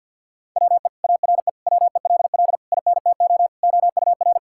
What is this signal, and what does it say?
Morse code, heard as a keyed tone, about 52 wpm, 700 Hz. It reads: WE ARE WELL INTO OUR
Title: Sentences from Top 1000 Words - Review Part 5 - 50wpm